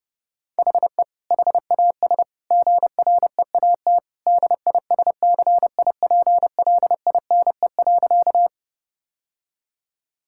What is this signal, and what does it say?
Morse code, heard as a keyed tone, about 32 wpm, 700 Hz. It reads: HE HAS GREAT DISCIPLINE.
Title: Sentences from Top 1000 Words - Review Part 6 - 30wpm